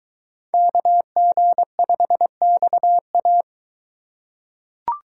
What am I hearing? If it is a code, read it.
Morse code, heard as a keyed tone, about 22 wpm, 700 Hz. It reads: KG5XA